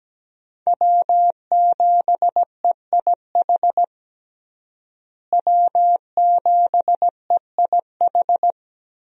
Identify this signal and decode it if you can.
Morse code, heard as a keyed tone, about 17 wpm, 700 Hz. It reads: W7EIH W7EIH
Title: USA Call Signs - 5 Characters - Repeated 2x - 17wpm